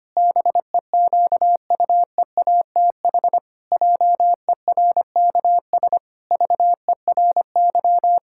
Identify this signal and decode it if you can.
Morse code, heard as a keyed tone, about 25 wpm, 700 Hz. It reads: BEQUEATH JERKS VERY
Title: Pangram Challenge v2 - 25wpm